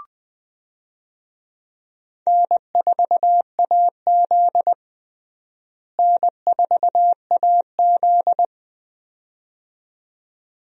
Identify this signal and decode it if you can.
Morse code, heard as a keyed tone, about 20 wpm, 700 Hz. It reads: N4AZ N4AZ
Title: USA Call Signs - 4 Characters - Repeated 2x - 20wpm